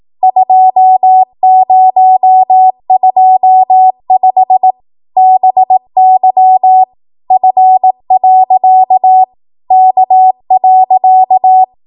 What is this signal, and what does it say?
Morse code, heard as a keyed tone, about 18 wpm, 750 Hz. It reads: 2025 BY F. K.
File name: Morse / 18WPM